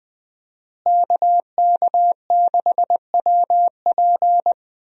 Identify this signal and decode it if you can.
Morse code, heard as a keyed tone, about 20 wpm, 700 Hz. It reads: KK6WP
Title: USA Call Signs - 5 Characters - Repeated 3x - 20wpm